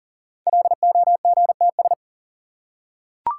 Morse code, heard as a keyed tone, about 40 wpm, 690 Hz. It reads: LOGTS
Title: Code Groups - 5 Letter:Numbers - 40wpm